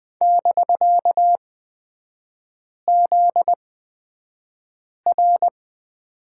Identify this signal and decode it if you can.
Morse code, heard as a keyed tone, about 20 wpm, spaced as for 5 wpm, 700 Hz. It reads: <BK>ZR
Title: 250 Lesson 40e - BK - Three Characters 20wpm